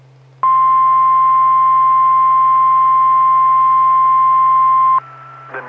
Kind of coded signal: Emergency Alert System (SAME) tones